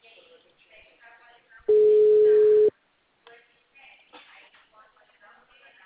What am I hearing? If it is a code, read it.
Ringback tone (Europe, 425 Hz), with speech.